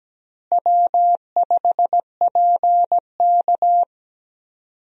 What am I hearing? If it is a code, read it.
Morse code, heard as a keyed tone, about 17 wpm, 700 Hz. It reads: W5PK
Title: USA Call Signs - 4 Characters - Repeated 3x - 17wpm